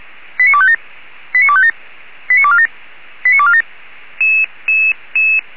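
Five-tone selective call (ZVEI), heard as quick runs of five tones, tones of about 80 ms. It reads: 89257 89257 89247 89247
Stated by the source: ZVEI five-tone call received by SDR, feuerwehr-orlamuende.org (fire brigade, Orlamünde).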